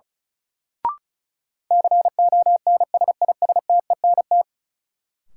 Morse code, heard as a keyed tone, about 36 wpm, 700 Hz. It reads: CONSISTENT
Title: Sentences from Top 1500 Words - New Word Drill - 35wpm